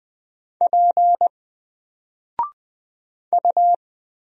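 Morse code, heard as a keyed tone, about 20 wpm, 700 Hz. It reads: P U
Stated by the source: Single letter and number Morse drill at 20 wpm, from N4LCD.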